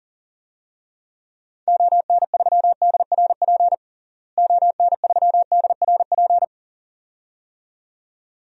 Morse code, heard as a keyed tone, about 37 wpm, 700 Hz. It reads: ON3DRP ON3DRP
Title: DXCC-100 Call Signs - 6 Characters - Repeated 2x - 40wpm